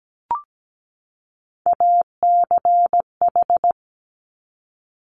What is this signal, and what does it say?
Morse code, heard as a keyed tone, about 17 wpm, 700 Hz. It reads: ACH
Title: Top Three Letter Combinations in Words 17wpm